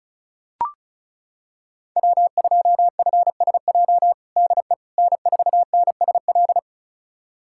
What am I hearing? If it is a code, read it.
Morse code, heard as a keyed tone, about 36 wpm, 700 Hz. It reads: W2FSJ DE N4NSL
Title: USA Callsign DE Callsign - 5 Characters - 35wpm